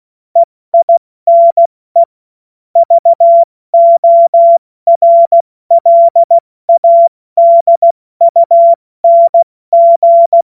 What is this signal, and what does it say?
Morse code, heard as a keyed tone, about 16 wpm, 680 Hz. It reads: EINE VORLADUNG
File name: Morsekurs / Witze / Witz 14.November 2023 70 F80